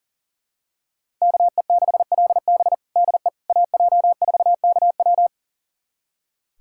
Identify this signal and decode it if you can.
Morse code, heard as a keyed tone, about 37 wpm, 700 Hz. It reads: KE6LB DE AJ4KW
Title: USA Callsign DE Callsign - 5 Characters - 40wpm